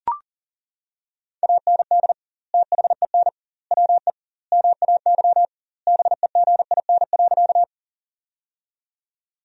Morse code, heard as a keyed tone, about 40 wpm, 700 Hz. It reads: AND THEN WE MAY BEGIN.
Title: 2x Word-Spacing - Sentences from Top 300 Words - 40wpm